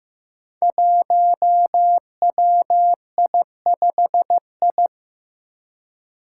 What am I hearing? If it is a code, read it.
Morse code, heard as a keyed tone, about 15 wpm, 700 Hz. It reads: 1WI5I
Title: Five Challenging Letter-Number Combinations 15wpm